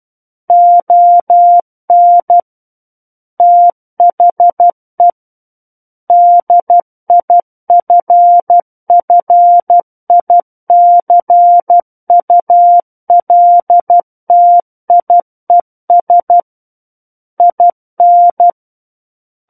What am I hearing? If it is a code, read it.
Morse code, heard as a keyed tone, about 12 wpm, 700 Hz. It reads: ON THE DIFFICULTIES IN
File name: Book One / 03-Chapter 3 - 12 WPM